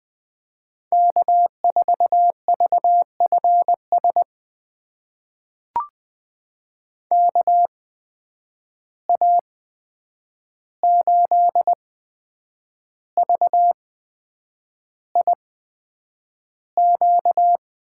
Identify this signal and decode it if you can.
Morse code, heard as a keyed tone, about 20 wpm, spaced as for 8 wpm, 700 Hz. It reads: K4VFS KA8VIQ
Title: USA Call Signs - 8x Character Spacing - 20wpm